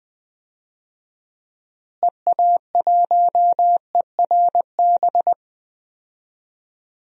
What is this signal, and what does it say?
Morse code, heard as a keyed tone, about 20 wpm, 700 Hz. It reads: EA1ERB
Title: DXCC-100 Call Signs - 6 Characters - Repeated 3x - 20wpm